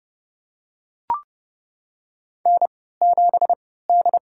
Morse code, heard as a keyed tone, about 30 wpm, spaced as for 17 wpm, 700 Hz. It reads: N7D
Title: USA Call Signs - 3x Character Spacing - 30wpm